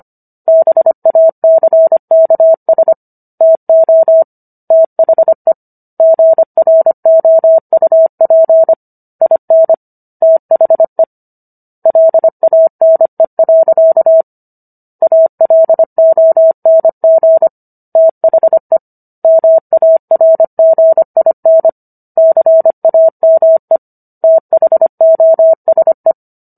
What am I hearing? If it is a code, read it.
Morse code, heard as a keyed tone, about 25 wpm, 650 Hz. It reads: BACKS TO THE GROUP IN THE LANE. ALONG THE MARGIN CAME THOSE